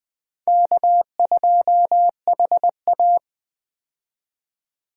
Morse code, heard as a keyed tone, about 20 wpm, 700 Hz. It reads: K2HA
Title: 2023 POTA Hunter Call Signs - 20wpm